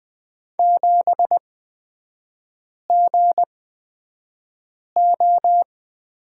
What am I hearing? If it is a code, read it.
Morse code, heard as a keyed tone, about 20 wpm, spaced as for 5 wpm, 690 Hz. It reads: 7GO
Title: 236 Lesson 38e - Letter q - Three Characters 20wpm